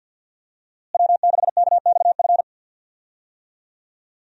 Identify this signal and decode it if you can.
Morse code, heard as a keyed tone, about 52 wpm, 700 Hz. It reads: W6KXF
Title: USA Call Signs - 5 Characters - Repeated 2x - 50wpm